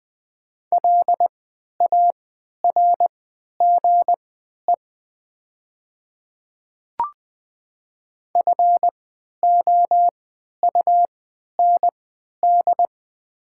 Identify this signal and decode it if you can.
Morse code, heard as a keyed tone, about 20 wpm, spaced as for 11 wpm, 700 Hz. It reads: LARGE FOUND
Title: 3x Character Spacing - Top 500 Words - 20wpm